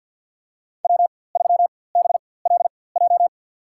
Morse code, heard as a keyed tone, about 47 wpm, spaced as for 23 wpm, 700 Hz. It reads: W3BLJ
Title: USA Call Signs - 5 Characters - 4x Character Spacing - 50wpm